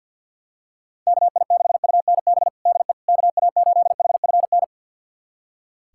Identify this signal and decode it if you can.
Morse code, heard as a keyed tone, about 46 wpm, 700 Hz. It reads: KI6UNB DE KA8HFN
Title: USA Callsign DE Callsign - 6 Characters - 50wpm